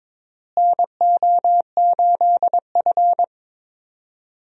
Morse code, heard as a keyed tone, about 22 wpm, 700 Hz. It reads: NO8F
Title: USA Call Signs - 4 Characters - Repeated 3x - 22wpm